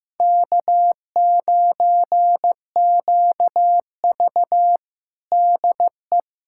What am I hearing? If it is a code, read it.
Morse code, heard as a keyed tone, about 15 wpm, 700 Hz. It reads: K9QV DE
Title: USA Callsign DE Callsign - 4 Characters - 15wpm